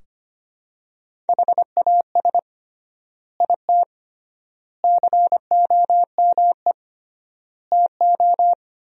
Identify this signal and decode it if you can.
Morse code, heard as a keyed tone, about 25 wpm, 700 Hz. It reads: HAS IT COME TO
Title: 3x Word-Spacing - Sentences from Top 100 Words - 25wpm